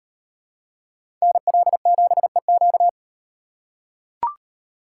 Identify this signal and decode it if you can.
Morse code, heard as a keyed tone, about 38 wpm, 700 Hz. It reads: NL7EQ